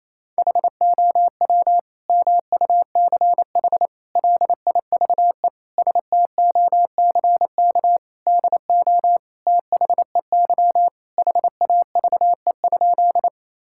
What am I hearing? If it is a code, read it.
Morse code, heard as a keyed tone, about 28 wpm, 700 Hz. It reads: HOW MUCH LIVESTOCK DO THEY HAVE?
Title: Sentences from Top 1000 Words - Review Part 5 - 28wpm